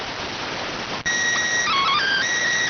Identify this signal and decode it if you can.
SSTV (slow-scan television), PD 290.